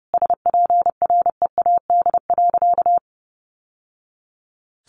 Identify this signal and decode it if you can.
Morse code, heard as a keyed tone, about 30 wpm, 700 Hz. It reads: SPREAD.
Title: Sentences from Top 1200 Words 30wpm